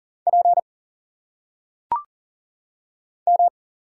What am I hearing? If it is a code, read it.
Morse code, heard as a keyed tone, about 38 wpm, 700 Hz. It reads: P M
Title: Single Letters - Phonetic Pronunciation 40wpm